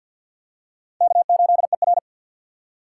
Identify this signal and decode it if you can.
Morse code, heard as a keyed tone, about 51 wpm, 700 Hz. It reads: K8ER
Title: USA Call Signs - 4 Characters - 3x Character Spacing - 50wpm